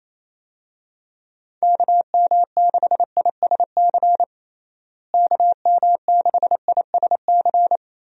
Morse code, heard as a keyed tone, about 27 wpm, 700 Hz. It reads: KM6ISC KM6ISC
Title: USA Call Signs - 6 Characters - Repeated 3x - 28wpm